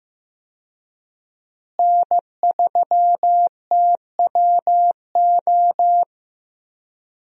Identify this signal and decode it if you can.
Morse code, heard as a keyed tone, about 16 wpm, 700 Hz. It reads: N3TWO
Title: USA Call Signs - 5 Characters - Repeated 2x - 15wpm